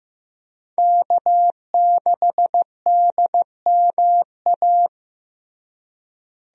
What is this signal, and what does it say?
Morse code, heard as a keyed tone, about 15 wpm, 700 Hz. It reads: K6DMA
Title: USA Call Signs - 5 Characters - Repeated 2x - 15wpm